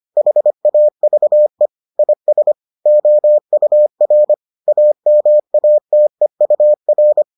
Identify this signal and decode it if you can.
Morse code, heard as a keyed tone, about 25 wpm, 600 Hz. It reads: HAVE IS OUR AMATEUR